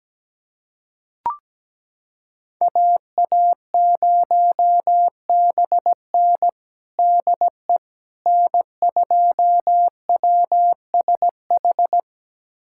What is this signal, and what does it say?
Morse code, heard as a keyed tone, about 17 wpm, 700 Hz. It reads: AA0BN DE N2WSH
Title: USA Callsign DE Callsign - 5 Characters - 17wpm